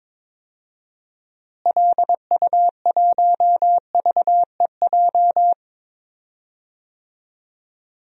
Morse code, heard as a keyed tone, about 21 wpm, 700 Hz. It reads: LU1VEJ
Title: DXCC-100 Call Signs - 6 Characters - Repeated 3x - 22wpm